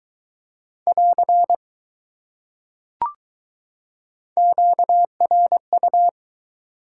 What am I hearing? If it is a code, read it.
Morse code, heard as a keyed tone, about 20 wpm, 700 Hz. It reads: <AR> QRU